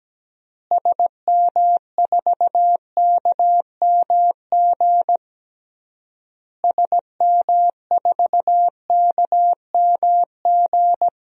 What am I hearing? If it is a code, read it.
Morse code, heard as a keyed tone, about 17 wpm, 700 Hz. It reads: SM4KMG SM4KMG
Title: DXCC-100 Call Signs - 6 Characters - Repeated 3x - 17wpm